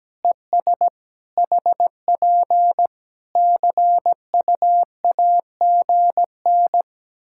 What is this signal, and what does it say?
Morse code, heard as a keyed tone, about 17 wpm, 700 Hz. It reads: ES HP CUAGN
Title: Practice QSO - Exchange 4 - Ending - 17wpm